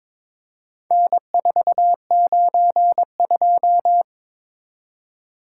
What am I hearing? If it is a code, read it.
Morse code, heard as a keyed tone, about 22 wpm, 700 Hz. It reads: N492
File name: practice / 22wpm / Four Letter-Number Combinations - 22wpm